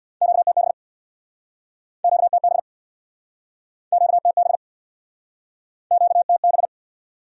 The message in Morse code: QTB QTB QTB QTB

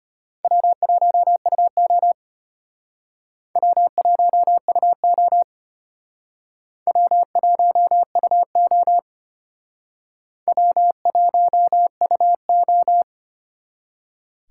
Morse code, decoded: W1UO W1UO W1UO W1UO

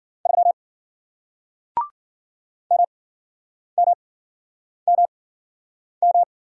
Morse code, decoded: 3 MMMM

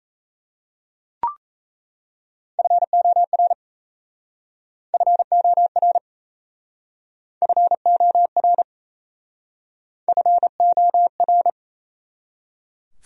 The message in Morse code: FOR FOR FOR FOR

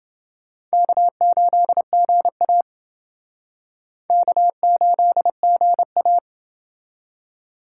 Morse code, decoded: K8GA K8GA

